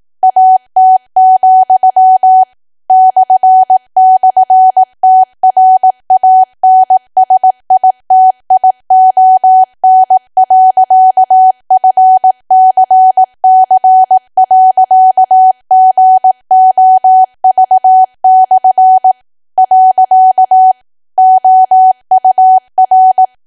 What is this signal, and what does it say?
Morse code, heard as a keyed tone, about 18 wpm, 750 Hz. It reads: AT, //TRANSITION.FCC.GOV/ . OUR
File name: cwa / W1AW-Level2 / 18WPM